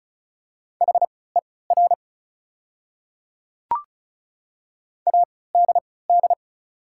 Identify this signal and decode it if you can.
Morse code, heard as a keyed tone, about 36 wpm, spaced as for 20 wpm, 700 Hz. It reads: HER ADD
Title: 3x Character Spacing - Top 200 Words - 35wpm